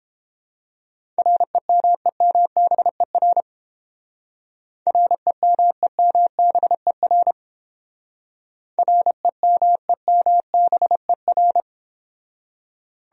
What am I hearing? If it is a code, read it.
Morse code, heard as a keyed tone, about 31 wpm, 700 Hz. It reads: REMEMBER REMEMBER REMEMBER